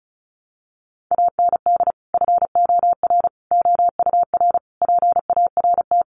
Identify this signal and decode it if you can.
Morse code, heard as a keyed tone, about 34 wpm, 700 Hz. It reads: AND FOR OUR PART
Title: Sentences from Top 1100 Words 35wpm